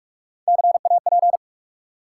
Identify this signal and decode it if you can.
Morse code, heard as a keyed tone, about 44 wpm, 700 Hz. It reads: CAP
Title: Three Letter Words - 45wpm